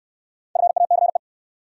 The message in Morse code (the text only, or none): FACE